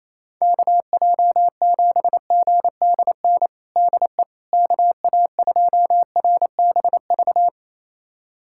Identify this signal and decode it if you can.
Morse code, heard as a keyed tone, about 28 wpm, 700 Hz. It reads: KJ7GDN DE KA2RBV
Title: USA Callsign DE Callsign - 6 Characters - 28wpm